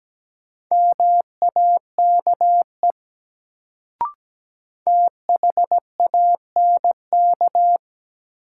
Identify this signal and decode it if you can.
Morse code, heard as a keyed tone, about 17 wpm, 700 Hz. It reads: MAKE THANK
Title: Sets of 2 Words - New Word Drill - 17wpm